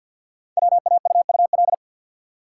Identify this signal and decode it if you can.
Morse code, heard as a keyed tone, about 48 wpm, 700 Hz. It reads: WAUUL
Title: Code Groups - 5 Letters - 50wpm